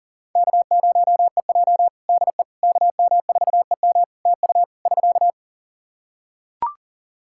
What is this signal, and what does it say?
Morse code, heard as a keyed tone, about 40 wpm, 700 Hz. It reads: K0EJ DE KM4EM TU <SK>
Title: Practice QSO - Exchange 4 - Ending - 40wpm